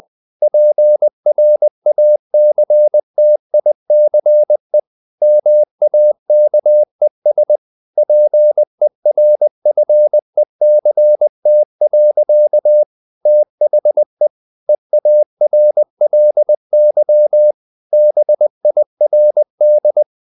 Morse code, heard as a keyed tone, about 20 wpm, 600 Hz. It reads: PRACTICE MAKES PERFECT. THE EARLY BIRD
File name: CW / Practice / EnglishSayings20